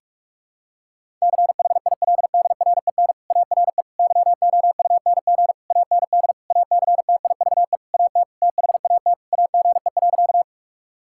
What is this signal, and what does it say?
Morse code, heard as a keyed tone, about 41 wpm, 700 Hz. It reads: CHILDREN ARE YOUNG AND ACTIVE AT THAT AGE.
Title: Sentences from Top 1000 Words - Review Part 6 - 45wpm